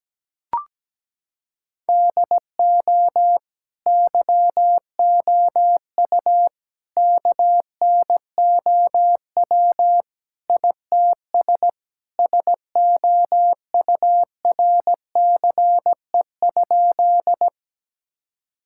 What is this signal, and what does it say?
Morse code, heard as a keyed tone, about 17 wpm, 700 Hz. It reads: DO YOU KNOW ITS SOURCE?
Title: Sentences from Top 1000 Words - Review Part 5 - 17wpm